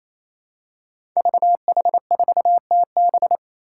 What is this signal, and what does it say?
Morse code, heard as a keyed tone, about 28 wpm, 700 Hz. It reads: VH4TB
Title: Five Challenging Letter-Number Combinations 28wpm